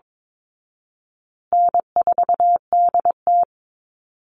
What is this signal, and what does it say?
Morse code, heard as a keyed tone, about 22 wpm, 700 Hz. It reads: N4DT
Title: Call Signs Repeated 2x - 22wpm